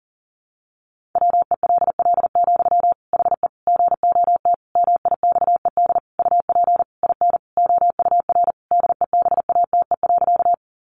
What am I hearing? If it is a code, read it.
Morse code, heard as a keyed tone, about 40 wpm, 700 Hz. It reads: WELL, HE GOT MIXED UP IN OUR DEBATE.
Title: Sentences from Top 1500 Words 40wpm